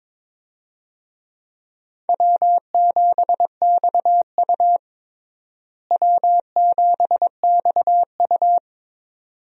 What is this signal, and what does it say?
Morse code, heard as a keyed tone, about 22 wpm, 700 Hz. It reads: W7XU W7XU
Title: USA Call Signs - 4 Characters - Repeated 2x - 22wpm